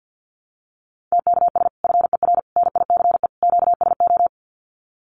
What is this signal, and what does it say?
Morse code, heard as a keyed tone, about 50 wpm, 700 Hz. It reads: TKS FER NICE QSO
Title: Common QSO Phrases 50wpm